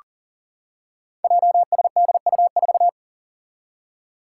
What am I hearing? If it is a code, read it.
Morse code, heard as a keyed tone, about 42 wpm, 690 Hz. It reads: JSDU4
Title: Five Challenging Letter-Number Combinations 40wpm